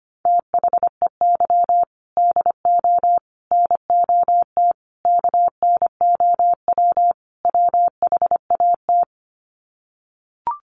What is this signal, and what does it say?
Morse code, heard as a keyed tone, about 25 wpm, 700 Hz. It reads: THEY DO NOT KNOW WHAT
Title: Sets of 5 Words Encore 25wpm